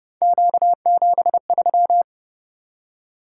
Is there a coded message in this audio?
Yes: Morse code from 0.2 to 2.0 s, heard as a keyed tone, about 29 wpm, 700 Hz. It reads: Q73